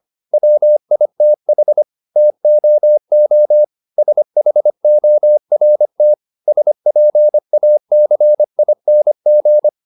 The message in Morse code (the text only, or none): WITH TOO SHORT SPACING